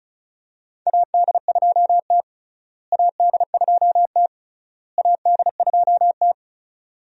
Morse code, heard as a keyed tone, about 36 wpm, 700 Hz. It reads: AD2T AD2T AD2T
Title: USA Call Signs - 4 Characters - Repeated 3x - 35wpm